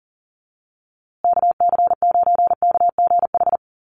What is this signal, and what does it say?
Morse code, heard as a keyed tone, about 40 wpm, 700 Hz. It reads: KC9KGH
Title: USA Call Signs Repeated 3x 40wpm